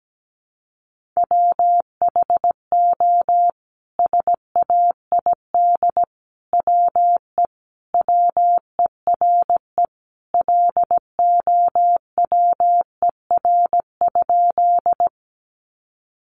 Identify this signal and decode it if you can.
Morse code, heard as a keyed tone, about 17 wpm, 700 Hz. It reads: WHO SAID WE WERE LOWER?
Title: Sentences from Top 700 Words 17wpm